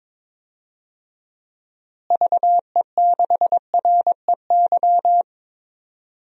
Morse code, heard as a keyed tone, about 22 wpm, 700 Hz. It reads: VE6REY
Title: DXCC-100 Call Signs - 6 Characters - Repeated 3x - 22wpm